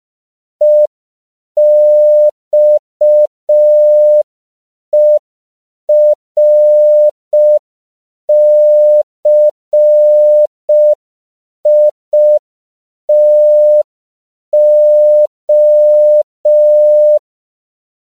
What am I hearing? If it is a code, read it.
Morse code, heard as a keyed tone, about 5 wpm, 600 Hz. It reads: EXERCITO
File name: CW / 5PPM